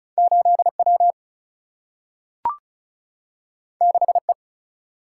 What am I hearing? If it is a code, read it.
Morse code, heard as a keyed tone, about 36 wpm, 700 Hz. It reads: 8 W 6 E
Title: Single-Letter Single-Number 35wpm